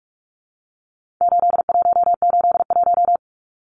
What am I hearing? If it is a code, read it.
Morse code, heard as a keyed tone, about 44 wpm, 700 Hz. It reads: 8181
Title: Numbers - 4-Digit 45wpm